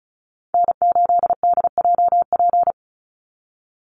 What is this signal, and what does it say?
Morse code, heard as a keyed tone, about 34 wpm, 700 Hz. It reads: N8DJP
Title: USA Call Signs 35wpm